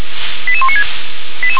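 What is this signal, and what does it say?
Five-tone selective call (ZVEI), heard as quick runs of five tones, tones of about 70 ms. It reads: 88196
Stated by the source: Five-tone ZVEI selective call 88164 received by SDR.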